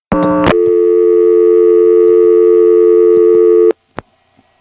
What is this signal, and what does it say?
Dial tone (North America), 350 + 440 Hz.